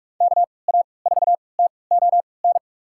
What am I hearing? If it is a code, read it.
Morse code, heard as a keyed tone, about 44 wpm, spaced as for 25 wpm, 700 Hz. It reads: KA4TON
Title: USA Call Signs - 3x Character Spacing - 45wpm